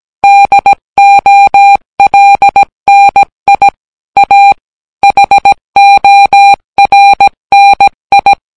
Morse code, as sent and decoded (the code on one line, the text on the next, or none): -.. --- .-.. -. .. / .- / .... --- .-. -. ..
DOLNI A HORNI